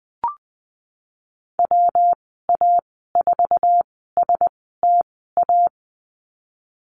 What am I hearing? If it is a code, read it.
Morse code, heard as a keyed tone, about 20 wpm, spaced as for 15 wpm, 700 Hz. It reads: WA4STA
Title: 186 Lesson 30g - Letter b - Call Signs 20wpm